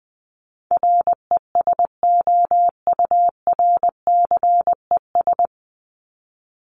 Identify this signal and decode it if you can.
Morse code, heard as a keyed tone, about 20 wpm, 700 Hz. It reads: RESOURCES
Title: ES:S Suffix - Top 100 Words with Base Words 20wpm